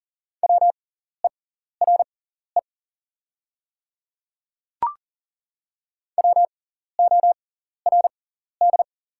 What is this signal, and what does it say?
Morse code, heard as a keyed tone, about 40 wpm, spaced as for 14 wpm, 700 Hz. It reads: WERE WORD
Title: 6x Character Spacing - Top 100 Words - 40wpm